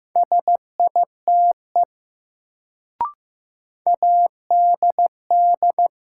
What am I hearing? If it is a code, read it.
Morse code, heard as a keyed tone, about 15 wpm, 700 Hz. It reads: SITE ADD
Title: Sentences from Top 1000 Words - Review Part 1 - New Word Drill - 15wpm